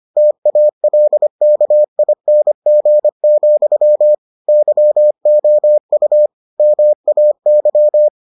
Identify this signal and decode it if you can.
Morse code, heard as a keyed tone, about 25 wpm, 600 Hz. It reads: TALKING, YOU MAY